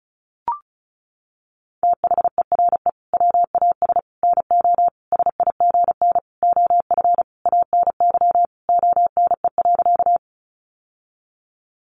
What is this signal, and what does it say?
Morse code, heard as a keyed tone, about 34 wpm, 700 Hz. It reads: THERE WAS NO SIGN OF ANY ONE.
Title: Sentences from Top 1000 Words 35wpm